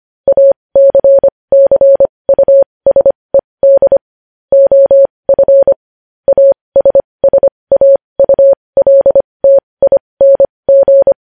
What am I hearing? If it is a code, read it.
Morse code, heard as a keyed tone, about 25 wpm, spaced as for 20 wpm, 550 Hz. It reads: ACCUSED OF ASSAULTING